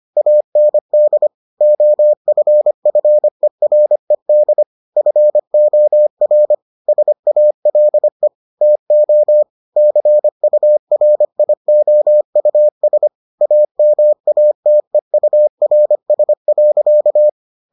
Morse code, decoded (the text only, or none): AND OFFERED FOR SALE TO CURIOUS AMATEURS.